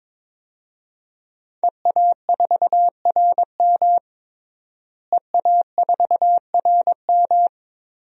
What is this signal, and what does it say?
Morse code, heard as a keyed tone, about 22 wpm, 700 Hz. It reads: EA4RM EA4RM
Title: DXCC-100 Call Signs - 5 Characters - Repeated 3x - 22wpm